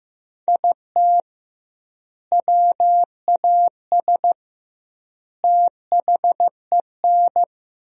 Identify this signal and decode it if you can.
Morse code, heard as a keyed tone, about 16 wpm, 700 Hz. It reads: IT WAS THEN
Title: 2x Word-Spacing - Sentences from Top 300 Words - 15wpm